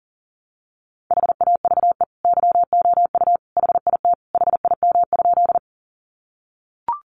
Morse code, heard as a keyed tone, about 40 wpm, 700 Hz. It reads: HAVE YOU HIT HIM?
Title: Sentences from Top 1200 Words 40wpm